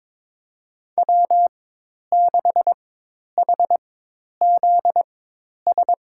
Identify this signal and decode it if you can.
Morse code, heard as a keyed tone, about 22 wpm, spaced as for 10 wpm, 700 Hz. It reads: W6HZS
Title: USA Call Signs - 5 Characters - 4x Character Spacing - 22wpm